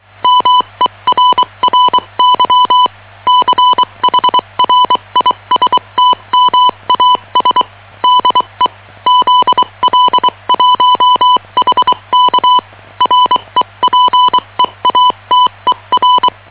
Morse code, heard as a keyed tone, about 24 wpm, spaced as for 21 wpm, 1000 Hz. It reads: MERRY CHRISTMAS DE ZL1HK REPEATER
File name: challenge122